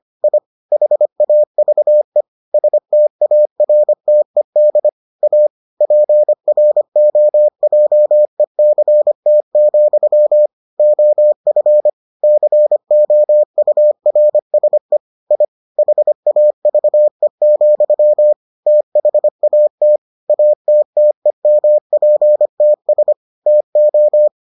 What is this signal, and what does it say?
Morse code, heard as a keyed tone, about 25 wpm, 600 Hz. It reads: I HAVE STARTED A PROJECT, OF COURSE I HAVE, THAT ATTEMPTS TO